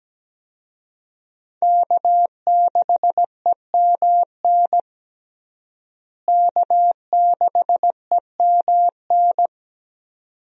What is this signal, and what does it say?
Morse code, heard as a keyed tone, about 17 wpm, 700 Hz. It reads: K6EMN K6EMN
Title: USA Call Signs - 5 Characters - Repeated 2x - 17wpm